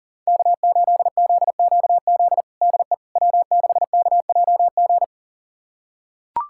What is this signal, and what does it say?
Morse code, heard as a keyed tone, about 40 wpm, 700 Hz. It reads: K8ZQZ DE W6KJG
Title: USA Callsign DE Callsign - 5 Characters - 40wpm